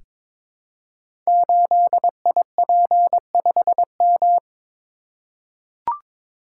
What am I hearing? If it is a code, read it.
Morse code, heard as a keyed tone, about 22 wpm, 700 Hz. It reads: 8IP5M
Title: Code Groups - 5 Letter:Numbers - 22wpm